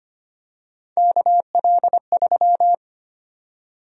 Morse code, heard as a keyed tone, about 25 wpm, 700 Hz. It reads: KL3
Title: USA Call Sign Prefixes - 25wpm